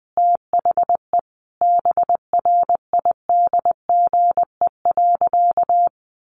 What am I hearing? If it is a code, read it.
Morse code, heard as a keyed tone, about 20 wpm, 700 Hz. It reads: THE BRIDGE.
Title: Sentences from Top 1500 Words 20wpm